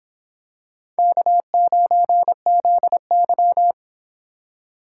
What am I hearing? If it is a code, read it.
Morse code, heard as a keyed tone, about 26 wpm, 700 Hz. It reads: K9ZY